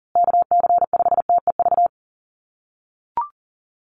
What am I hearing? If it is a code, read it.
Morse code, heard as a keyed tone, about 40 wpm, 700 Hz. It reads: KC5TEV
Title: USA Call Signs Repeated 2x 40wpm